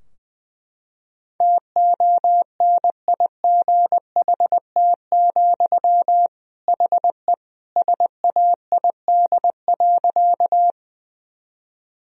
Morse code, heard as a keyed tone, about 20 wpm, 700 Hz. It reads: TONIGHT, HE SAID.
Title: Sentences from Top 1000 Words - Review Part 3 - 20wpm